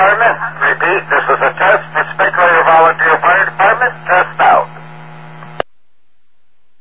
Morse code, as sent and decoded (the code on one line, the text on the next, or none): none